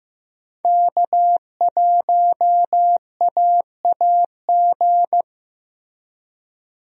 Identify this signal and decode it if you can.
Morse code, heard as a keyed tone, about 15 wpm, 700 Hz. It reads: K1AAG
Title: USA Call Signs - 5 Characters - Repeated 3x - 15wpm